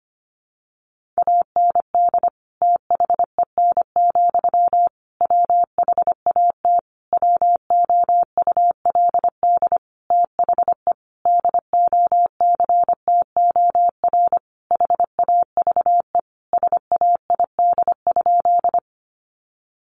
Morse code, heard as a keyed tone, about 25 wpm, 700 Hz. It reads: AND THEN, WHAT WOULD THE DOCTOR HAVE SAID?
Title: Sentences from Top 1500 Words 25wpm